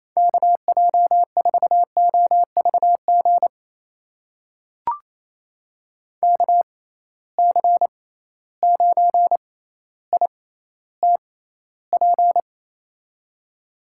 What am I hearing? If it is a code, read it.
Morse code, heard as a keyed tone, about 28 wpm, spaced as for 14 wpm, 700 Hz. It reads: KJ4OVG KC9ITP
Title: USA Call Signs - 6 Characters - 6x Character Spacing - 28wpm